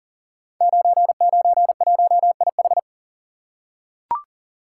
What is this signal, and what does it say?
Morse code, heard as a keyed tone, about 38 wpm, 700 Hz. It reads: 991IH
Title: Code Groups - 5 Letter:Numbers - 40wpm